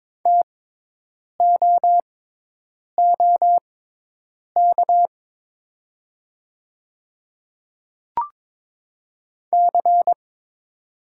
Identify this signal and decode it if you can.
Morse code, heard as a keyed tone, about 22 wpm, spaced as for 8 wpm, 700 Hz. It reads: TOOK C
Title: 6x Character Spacing - Top 300 Words - 22wpm